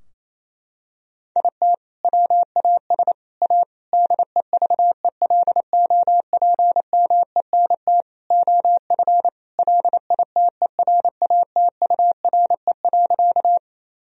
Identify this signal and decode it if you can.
Morse code, heard as a keyed tone, about 28 wpm, 700 Hz. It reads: IT WAS A DEVELOPMENT OF LITERATURE.
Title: Sentences from Top 1000 Words - Review Part 4 - 28wpm